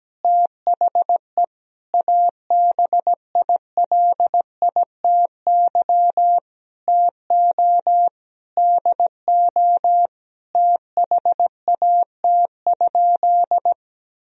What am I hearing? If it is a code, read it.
Morse code, heard as a keyed tone, about 17 wpm, 700 Hz. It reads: THE ABILITY TO DO THAT?
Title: Sentences from Top 1000 Words - Review Part 3 - 17wpm